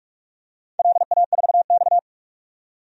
Morse code, heard as a keyed tone, about 45 wpm, 700 Hz. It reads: LA4X
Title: DXCC-100 Call Signs - 4 Characters - 45wpm